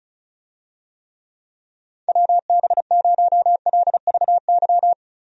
Morse code, heard as a keyed tone, about 36 wpm, 700 Hz. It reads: WB0LVY
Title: USA Call Signs - 6 Characters - Repeated 2x - 35wpm